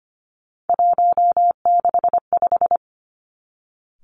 Morse code, heard as a keyed tone, about 25 wpm, 700 Hz. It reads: 165
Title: Three Challenging Letter-Number Combinations 25wpm